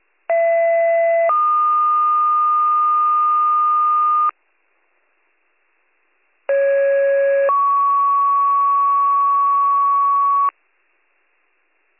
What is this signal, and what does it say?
Two-tone page, heard as a one-second tone, then a three-second tone, about 670 then 1220 Hz, about 550 then 1120 Hz. It is heard from 0.3 to 10.5 s.